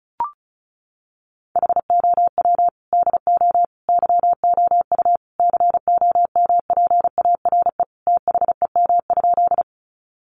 Morse code, heard as a keyed tone, about 34 wpm, 700 Hz. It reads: HOW DO YOU COMPARE THEM?
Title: Sentences from Top 1300 Words 35wpm